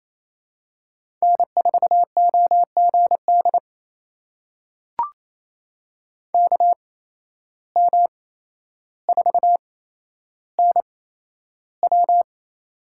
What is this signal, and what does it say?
Morse code, heard as a keyed tone, about 28 wpm, spaced as for 12 wpm, 700 Hz. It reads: N4OGD KM4NW
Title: USA Call Signs - 5 Characters - 8x Character Spacing - 28wpm